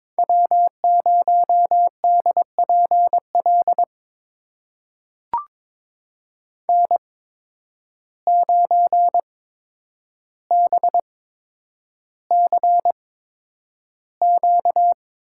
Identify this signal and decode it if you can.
Morse code, heard as a keyed tone, about 22 wpm, spaced as for 9 wpm, 700 Hz. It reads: W0DPL N9BCQ
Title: USA Call Signs - 8x Character Spacing - 22wpm